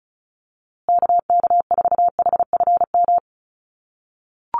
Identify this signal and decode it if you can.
Morse code, heard as a keyed tone, about 34 wpm, 700 Hz. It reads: KK4HFM
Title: USA Call Signs Repeated 3x 35wpm